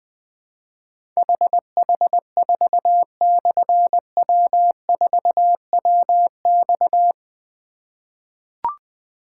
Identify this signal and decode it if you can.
Morse code, heard as a keyed tone, about 21 wpm, 700 Hz. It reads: HH4/W4WX
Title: Difficult Call Signs 20wpm